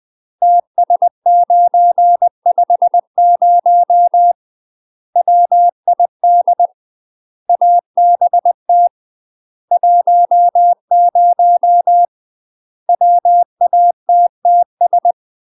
Morse code, heard as a keyed tone, about 20 wpm, 700 Hz. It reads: TS950 WID ABT 10 WATTS
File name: UK MASTER QSOs TRAINING / 20WPM / UK MASTER QSOs 0041